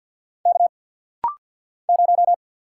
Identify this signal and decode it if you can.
Morse code, heard as a keyed tone, about 50 wpm, 700 Hz. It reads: K 0
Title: Single Letter-Number - Rapid-Fire 50wpm